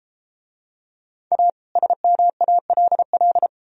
Morse code, heard as a keyed tone, about 31 wpm, 700 Hz. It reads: A SMALL